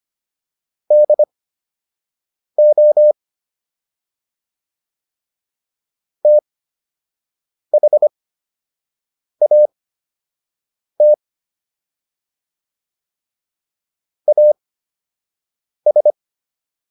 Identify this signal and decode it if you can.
Morse code, heard as a keyed tone, about 25 wpm, spaced as for 6 wpm, 600 Hz. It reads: DO THAT AS